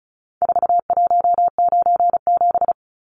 Morse code, heard as a keyed tone, about 34 wpm, 700 Hz. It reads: 4197
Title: Numbers - 4-Digit 35wpm